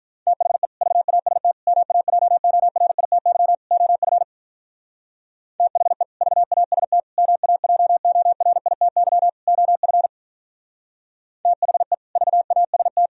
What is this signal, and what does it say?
Morse code, heard as a keyed tone, about 44 wpm, 700 Hz. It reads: THE VAST MAJORITY OF THE VAST MAJORITY OF THE VAST